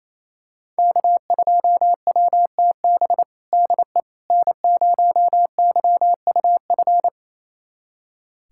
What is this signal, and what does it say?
Morse code, heard as a keyed tone, about 27 wpm, 700 Hz. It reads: K2WTB DE N0YUF
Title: USA Callsign DE Callsign - 5 Characters - 28wpm